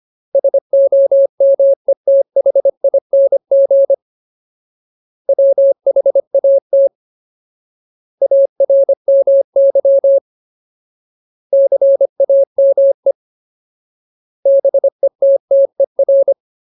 Morse code, heard as a keyed tone, about 25 wpm, 550 Hz. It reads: SOMETHING WHAT ARMY CAME BETTER
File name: Ultimate Word List / 25WPM Ultimate